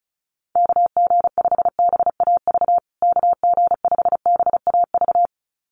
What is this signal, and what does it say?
Morse code, heard as a keyed tone, about 34 wpm, 700 Hz. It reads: KG5BAV KG5BAV
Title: USA Call Signs Repeated 2x 35wpm